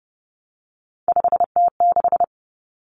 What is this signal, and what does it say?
Morse code, heard as a keyed tone, about 30 wpm, 700 Hz. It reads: 5T6
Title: Common Cut Numbers - 3-Digit 30wpm